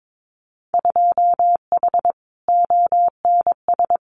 Morse code, heard as a keyed tone, about 22 wpm, 700 Hz. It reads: 2H ONS
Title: Winter Field Day Exchanges Encore 22wpm